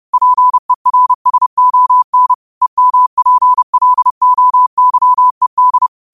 Morse code, heard as a keyed tone, about 30 wpm, 1000 Hz. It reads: PERSON EMPLOYED